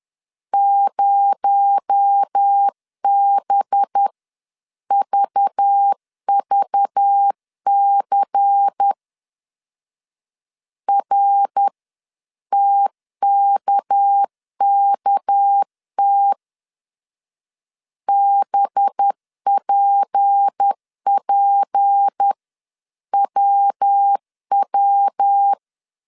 Morse code, heard as a keyed tone, about 10 wpm, spaced as for 9 wpm, 800 Hz. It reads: 0BVVC RTKKT BPPWW
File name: CW / DET / 1 6 1500C S B 10wpm 4137